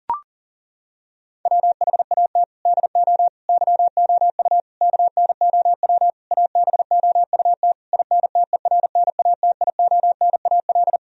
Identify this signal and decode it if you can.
Morse code, heard as a keyed tone, about 40 wpm, 700 Hz. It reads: WHAT DO YOU KNOW ABOUT INTERNATIONAL
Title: Sentences from Top 1000 Words - Review Part 4 - 40wpm